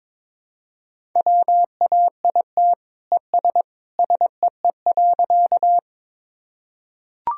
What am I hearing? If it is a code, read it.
Morse code, heard as a keyed tone, about 22 wpm, 700 Hz. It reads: WAIT ES SEE.
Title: Sentences from Top 500 Words and Abbreviations 22wpm